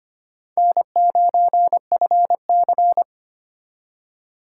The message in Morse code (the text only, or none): N9FC